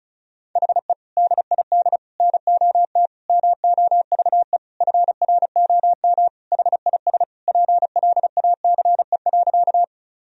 Morse code, heard as a keyed tone, about 36 wpm, 700 Hz. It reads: HE DID NOT MOVE FROM HIS PLACE.
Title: Sentences from Top 1000 Words - Review Part 1 - 35wpm